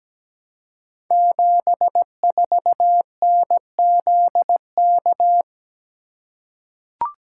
Morse code, heard as a keyed tone, about 17 wpm, 700 Hz. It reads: 74NZK
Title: Code Groups - 5 Letter:Numbers - 17wpm